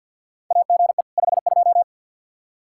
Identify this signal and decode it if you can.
Morse code, heard as a keyed tone, about 49 wpm, 700 Hz. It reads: AGE 52
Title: Practice QSO - Exchange 3 - More Info - 50wpm